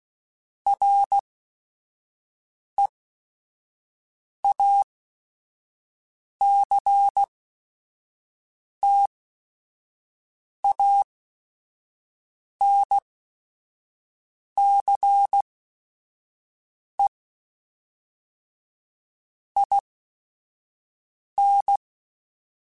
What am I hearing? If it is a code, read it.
Morse code, heard as a keyed tone, about 16 wpm, spaced as for 5 wpm, 800 Hz. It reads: REACTANCE IN
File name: Technical Subjects / Technical Subjects-mp3s / XL @5